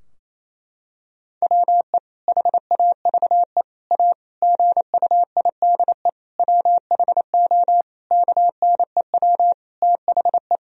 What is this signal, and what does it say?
Morse code, heard as a keyed tone, about 28 wpm, 700 Hz. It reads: WE HAVE A GUIDE WHO KNEW THE
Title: Sentences from Top 1000 Words - Review Part 3 - 28wpm